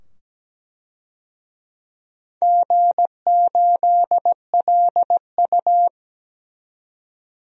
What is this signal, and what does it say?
Morse code, heard as a keyed tone, about 17 wpm, 700 Hz. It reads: G8LU
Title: DXCC-100 Call Signs - 4 Characters - 17wpm